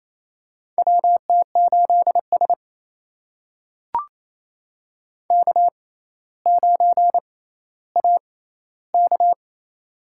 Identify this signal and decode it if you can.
Morse code, heard as a keyed tone, about 28 wpm, spaced as for 14 wpm, 700 Hz. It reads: WT8S K9AK
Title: USA Call Signs - 4 Characters - 6x Character Spacing - 28wpm